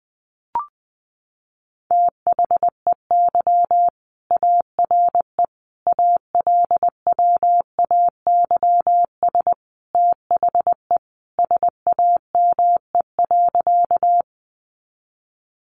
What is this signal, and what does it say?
Morse code, heard as a keyed tone, about 20 wpm, 700 Hz. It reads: THEY ARE ALWAYS THE SAME.
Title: Sentences from Top 300 Words 20wpm